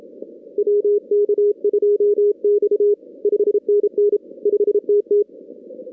Morse code, heard as a keyed tone, about 29 wpm, 410 Hz. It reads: WK2X 5NN 5TT
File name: 2016 arrl dx cw / 2016-ARRL-DX-CW-K5TR- - 0-46